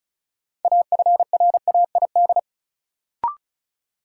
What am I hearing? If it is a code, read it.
Morse code, heard as a keyed tone, about 36 wpm, 700 Hz. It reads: AFRAID
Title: Sentences from Top 1500 Words - New Word Drill - 35wpm